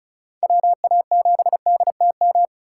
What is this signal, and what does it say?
Morse code, heard as a keyed tone, about 36 wpm, 690 Hz. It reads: WA7DTM
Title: USA Call Signs - 6 Characters - 6x Character Spacing - 35wpm